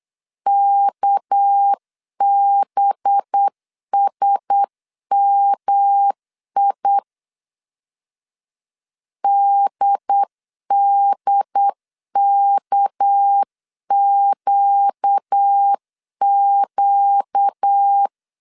Morse code, heard as a keyed tone, about 8 wpm, 800 Hz. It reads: KBSMI DDKQQ
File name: CW / DET / 1 6 1500C S C 8wpm 5219